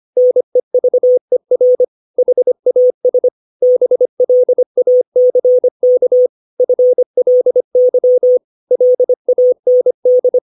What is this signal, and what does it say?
Morse code, heard as a keyed tone, about 25 wpm, 500 Hz. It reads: NEVER HAS BLACK FLY LAND
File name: 632 words / 632 Words 25WPM